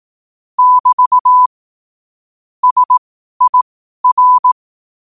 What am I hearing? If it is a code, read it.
Morse code, heard as a keyed tone, about 18 wpm, spaced as for 12 wpm, 1000 Hz. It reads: <BT> SIR
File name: qotdfiles / 12wpm